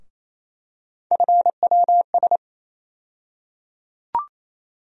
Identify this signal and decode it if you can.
Morse code, heard as a keyed tone, about 28 wpm, 700 Hz. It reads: FWS
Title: USA Call Sign Postfixes - 3 Letters 28wpm